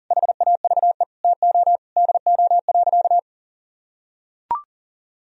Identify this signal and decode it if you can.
Morse code, heard as a keyed tone, about 40 wpm, 700 Hz. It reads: HAVE TO DO.
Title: Sentences from Top 1000 Words - Review Part 3 - 40wpm